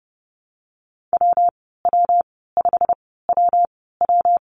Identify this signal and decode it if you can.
Morse code, heard as a keyed tone, about 30 wpm, spaced as for 17 wpm, 700 Hz. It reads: WW5WW
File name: Lesson 19g - Letter w - Call Signs 30wpm